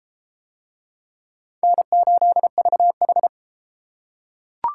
Morse code, heard as a keyed tone, about 30 wpm, 700 Hz. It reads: N8VH